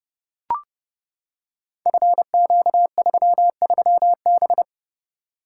Morse code, heard as a keyed tone, about 30 wpm, 700 Hz. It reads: FQ33B
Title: Code Groups - 5 Letter:Numbers - 30wpm